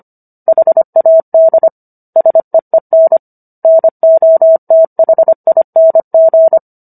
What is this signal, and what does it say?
Morse code, heard as a keyed tone, about 25 wpm, 650 Hz. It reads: HAD SEEN NOTHING
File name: Book One / 16-Chapter 16 - 25 WPM